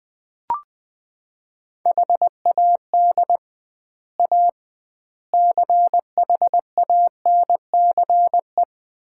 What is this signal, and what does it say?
Morse code, heard as a keyed tone, about 21 wpm, 700 Hz. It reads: HAD A CHANCE
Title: 2x Word-Spacing - Sets of 4 Words - 20wpm